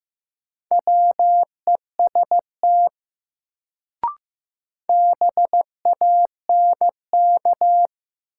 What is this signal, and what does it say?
Morse code, heard as a keyed tone, about 15 wpm, 700 Hz. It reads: WEST BANK
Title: Sets of 5 Words - New Word Drill - 15wpm